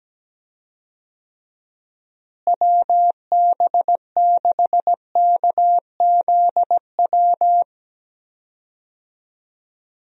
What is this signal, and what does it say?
Morse code, heard as a keyed tone, about 17 wpm, 700 Hz. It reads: WB6KZW
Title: USA Call Signs - 6 Characters - Repeated 3x - 17wpm